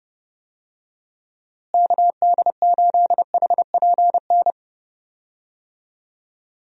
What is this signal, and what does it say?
Morse code, heard as a keyed tone, about 30 wpm, 700 Hz. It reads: KD8HPN
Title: USA Call Signs - 6 Characters - Repeated 3x - 30wpm